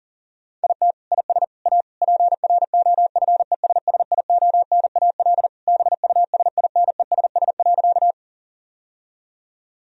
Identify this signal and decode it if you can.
Morse code, heard as a keyed tone, about 37 wpm, 700 Hz. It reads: IT IS A PROFESSIONAL BUSINESS.
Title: Sentences from Top 1000 Words - Review Part 4 - 40wpm